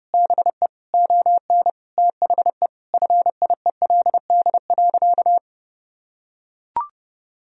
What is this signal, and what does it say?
Morse code, heard as a keyed tone, about 30 wpm, 700 Hz. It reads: BE ON THE FIELD.
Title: Sentences from Top 1000 Words - Review Part 2 - 30wpm